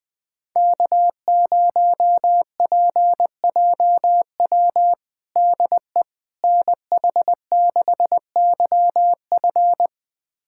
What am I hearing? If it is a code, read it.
Morse code, heard as a keyed tone, about 20 wpm, 700 Hz. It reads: K0PJW DE NH6YF
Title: USA Callsign DE Callsign - 5 Characters - 20wpm